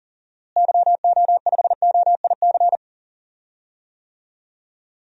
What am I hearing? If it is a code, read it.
Morse code, heard as a keyed tone, about 40 wpm, 700 Hz. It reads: YO5OIC
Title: DXCC-100 Call Signs - 6 Characters - Repeated 2x - 40wpm